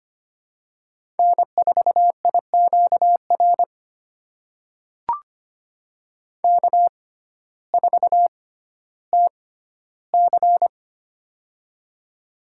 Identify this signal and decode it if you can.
Morse code, heard as a keyed tone, about 25 wpm, spaced as for 14 wpm, 700 Hz. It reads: N4IQR K4TC
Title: USA Call Signs - 6x Character Spacing - 25wpm